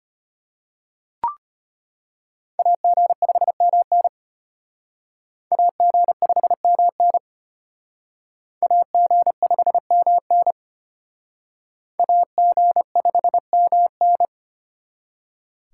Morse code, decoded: AG5MN AG5MN AG5MN AG5MN

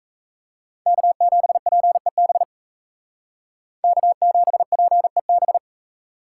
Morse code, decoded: K7PEB K7PEB

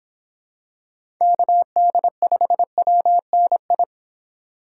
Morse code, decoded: KD5WNI

